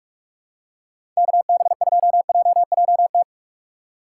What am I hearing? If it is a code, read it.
Morse code, heard as a keyed tone, about 44 wpm, 700 Hz. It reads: KB2JJT
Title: USA Call Signs - 6 Characters - 6x Character Spacing - 45wpm